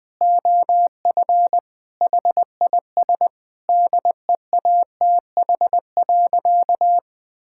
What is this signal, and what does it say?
Morse code, heard as a keyed tone, about 20 wpm, 700 Hz. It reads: OF HIS DEATH.
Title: Sentences with 6-Word Phrases - 20wpm